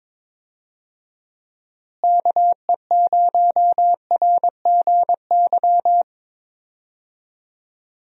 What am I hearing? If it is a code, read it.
Morse code, heard as a keyed tone, about 21 wpm, 700 Hz. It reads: KE0RGY
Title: USA Call Signs - 6 Characters - Repeated 2x - 22wpm